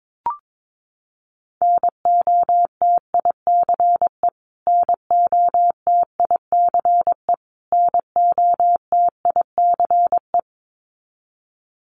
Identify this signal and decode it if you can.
Morse code, heard as a keyed tone, about 22 wpm, 700 Hz. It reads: NOTICE NOTICE NOTICE
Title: Top 500 Words Repeated 22wpm